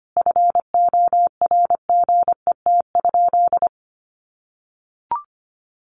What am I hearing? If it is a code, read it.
Morse code, heard as a keyed tone, about 25 wpm, 700 Hz. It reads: FORGET?
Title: Sentences from Top 1300 Words 25wpm